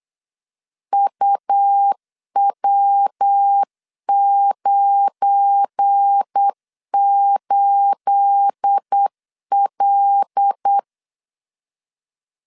Morse code, heard as a keyed tone, about 8 wpm, 800 Hz. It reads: UW98L
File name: LEZIONI / 9 28 8wpm 3512